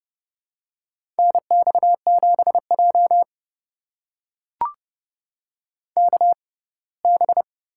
Morse code, heard as a keyed tone, about 30 wpm, spaced as for 20 wpm, 700 Hz. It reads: NX7J KB
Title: USA Call Signs - 6x Character Spacing - 30wpm